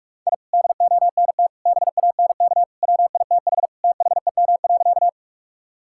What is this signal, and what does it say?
Morse code, heard as a keyed tone, about 46 wpm, 700 Hz. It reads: I DONT BANK WITH THEM.
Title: Sentences from Top 1000 Words - Review Part 6 - 45wpm